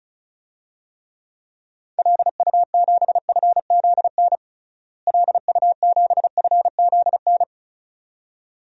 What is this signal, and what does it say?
Morse code, heard as a keyed tone, about 34 wpm, 700 Hz. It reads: LU7FZN LU7FZN
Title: DXCC-100 Call Signs - 6 Characters - Repeated 2x - 35wpm